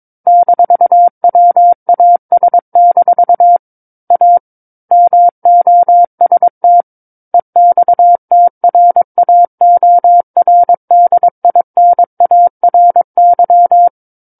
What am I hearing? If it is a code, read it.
Morse code, heard as a keyed tone, about 22 wpm, 700 Hz. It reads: -WAS- A MOST EXTRAORDINARY
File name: 06-Chapter-06 22 WPM